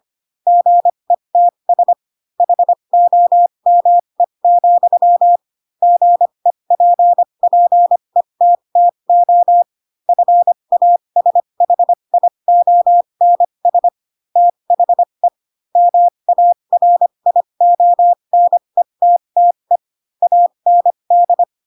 Morse code, heard as a keyed tone, about 25 wpm, spaced as for 22 wpm, 700 Hz. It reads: GETS HOME, GEPPETTO FASHIONS THE MARIONETTE AND
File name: Pinocchio / Pinocchio ch3 18 25 ik1hge